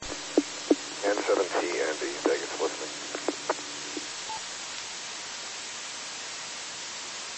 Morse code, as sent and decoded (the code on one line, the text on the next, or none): none